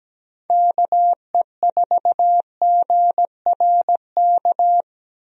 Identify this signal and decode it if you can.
Morse code, heard as a keyed tone, about 17 wpm, 700 Hz. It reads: KE4GRK
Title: USA Call Signs - 6 Characters - Repeated 2x - 17wpm